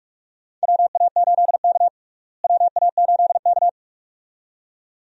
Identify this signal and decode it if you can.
Morse code, heard as a keyed tone, about 42 wpm, 700 Hz. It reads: WA8K WA8K
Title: USA Call Signs - 4 Characters - Repeated 2x - 45wpm